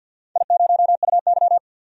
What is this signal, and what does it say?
Morse code, heard as a keyed tone, about 48 wpm, 700 Hz. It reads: I0UY
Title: DXCC-100 Call Signs - 4 Characters - 50wpm